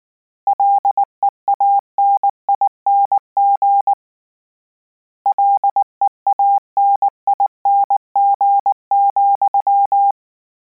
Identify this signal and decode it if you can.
Morse code, heard as a keyed tone, about 19 wpm, 800 Hz. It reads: LEANING LEANING,
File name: Combinations / Combinations-mp3s / Combinations@18